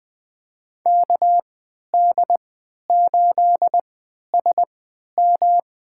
Morse code, heard as a keyed tone, about 20 wpm, spaced as for 11 wpm, 700 Hz. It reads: KD8SM
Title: USA Call Signs - 3x Character Spacing - 20wpm